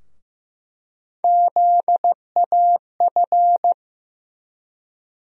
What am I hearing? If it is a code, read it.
Morse code, heard as a keyed tone, about 15 wpm, 700 Hz. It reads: ZAF
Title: USA Call Sign Postfixes - 3 Letters 15wpm